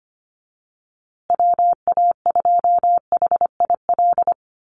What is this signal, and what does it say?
Morse code, heard as a keyed tone, about 25 wpm, 700 Hz. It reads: WA2HIL
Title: USA Call Signs Repeated 3x 25wpm